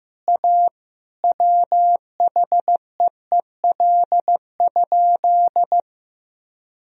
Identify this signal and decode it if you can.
Morse code, heard as a keyed tone, about 16 wpm, 700 Hz. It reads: A WHEEL?
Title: Sentences from Top 1000 Words - Review Part 3 - 15wpm